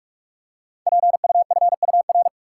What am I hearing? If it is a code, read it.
Morse code, heard as a keyed tone, about 44 wpm, 690 Hz. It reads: PUFUR
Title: Code Groups - 5 Letters - 45wpm